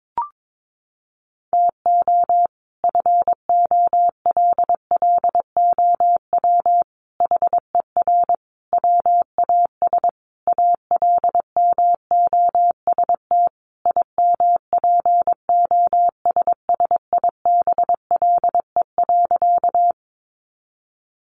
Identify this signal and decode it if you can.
Morse code, heard as a keyed tone, about 22 wpm, 700 Hz. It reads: TO FOLLOW HER WAS ALMOST IMPOSSIBLE.
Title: Sentences from Top 1100 Words 22wpm